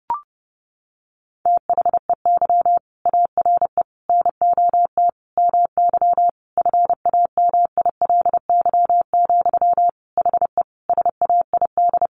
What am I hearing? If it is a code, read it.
Morse code, heard as a keyed tone, about 30 wpm, 700 Hz. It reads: THEY ARE NOT MY FAMILY, HE SAID
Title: Sentences from Top 1100 Words 30wpm